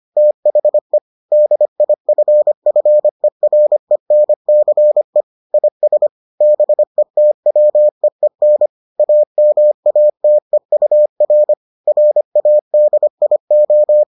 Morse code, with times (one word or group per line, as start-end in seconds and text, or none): 0.2-1.0 THE
1.3-5.2 DIFFERENCE
5.5-6.1 IS
6.4-8.7 BETWEEN
9.0-11.5 AMATEUR
11.9-14.0 RADIO